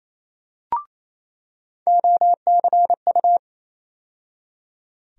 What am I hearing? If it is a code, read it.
Morse code, heard as a keyed tone, about 28 wpm, 700 Hz. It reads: OCU
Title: USA Call Sign Postfixes - 3 Letters 28wpm